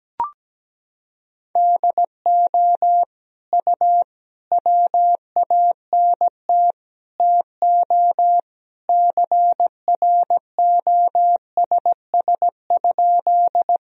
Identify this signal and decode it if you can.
Morse code, heard as a keyed tone, about 17 wpm, 700 Hz. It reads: DO U WANT TO CROSS?
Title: Sentences from Top 300 Words and Abbreviations 17wpm